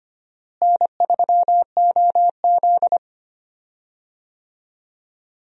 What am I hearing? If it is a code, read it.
Morse code, heard as a keyed tone, about 25 wpm, 700 Hz. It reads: N3OZ
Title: USA Call Signs - 4 Characters - Repeated 2x - 25wpm